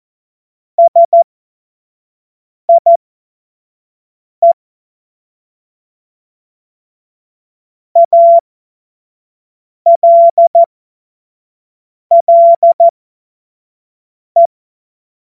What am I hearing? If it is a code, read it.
Morse code, heard as a keyed tone, about 14 wpm, spaced as for 5 wpm, 680 Hz. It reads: SIE ALLE
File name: Witz-23-September-2025-25-F70